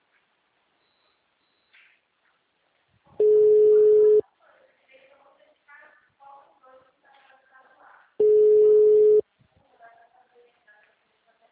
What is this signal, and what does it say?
Ringback tone (Europe, 425 Hz), with speech.
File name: out- -2026- - - .39